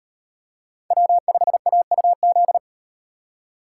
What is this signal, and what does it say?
Morse code, heard as a keyed tone, about 37 wpm, 700 Hz. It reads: W5AUZ